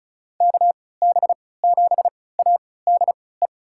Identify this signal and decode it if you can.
Morse code, heard as a keyed tone, about 36 wpm, spaced as for 20 wpm, 700 Hz. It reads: KB7ADE
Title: USA Call Signs - 6 Characters - 3x Character Spacing - 35wpm